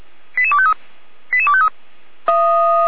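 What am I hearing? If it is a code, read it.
Five-tone selective call (ZVEI), heard as quick runs of five tones, tones of about 70 ms. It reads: 88253 88253
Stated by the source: ZVEI five-tone selective call received on SDR.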